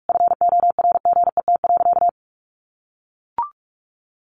Morse code, heard as a keyed tone, about 44 wpm, 700 Hz. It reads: FORGET.
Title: Sentences from Top 1500 Words 45wpm